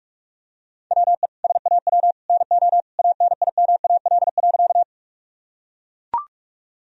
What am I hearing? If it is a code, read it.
Morse code, heard as a keyed tone, about 44 wpm, 700 Hz. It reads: WE SAW NO ANIMAL.
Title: Sentences from Top 1000 Words - Review Part 1 - 45wpm